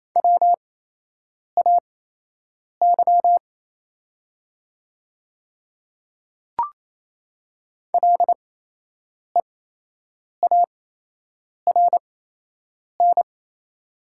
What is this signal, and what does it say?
Morse code, heard as a keyed tone, about 28 wpm, spaced as for 8 wpm, 700 Hz. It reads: WAY LEARN
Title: 8x Character Spacing - Top 300 Words - 28wpm